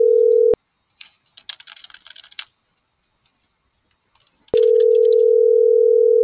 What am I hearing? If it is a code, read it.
Ringback tone (North America), 440 + 480 Hz.